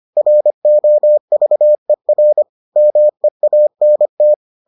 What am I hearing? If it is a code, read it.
Morse code, heard as a keyed tone, about 25 wpm, 600 Hz. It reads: ROVER MEANT